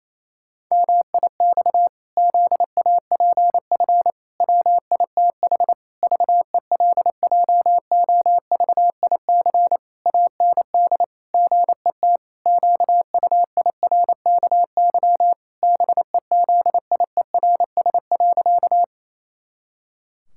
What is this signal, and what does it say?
Morse code, heard as a keyed tone, about 27 wpm, 700 Hz. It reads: MIX ZAPF WITH VELJOVIC AND GET QUIRKY BEZIERS.
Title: Pangram Challenge v2 - 28wpm